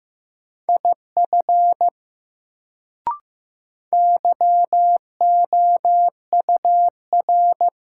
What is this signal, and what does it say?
Morse code, heard as a keyed tone, about 16 wpm, 700 Hz. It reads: IF YOUR
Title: Sentences from Top 100 Words - New Word Drill - 15wpm